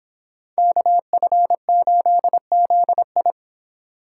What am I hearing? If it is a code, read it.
Morse code, heard as a keyed tone, about 26 wpm, 700 Hz. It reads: KF8ZI